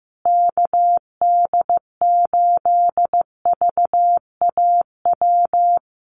Morse code, heard as a keyed tone, about 15 wpm, 700 Hz. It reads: KD8VAW
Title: USA Call Signs Repeated 3x 15wpm